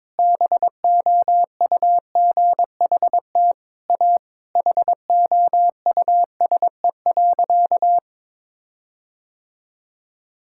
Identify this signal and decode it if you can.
Morse code, heard as a keyed tone, about 22 wpm, 700 Hz. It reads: BOUGHT A HOUSE.
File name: practice / 22wpm / Sentences with 4-Word Phrases - 22wpm